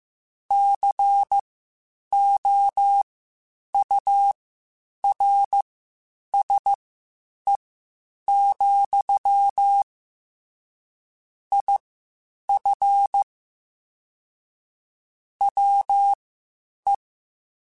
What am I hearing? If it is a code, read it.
Morse code, heard as a keyed tone, about 15 wpm, spaced as for 8 wpm, 800 Hz. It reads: COURSE, IF WE
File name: (spaced) / QSOs Spaced-mp3s / WA7QOX 1 @13 sp